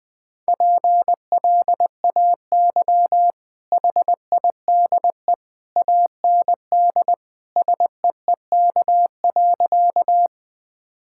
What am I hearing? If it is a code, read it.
Morse code, heard as a keyed tone, about 20 wpm, 700 Hz. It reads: PLAY HIDE AND SEEK.
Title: Easy Sentences with Hints - 20wpm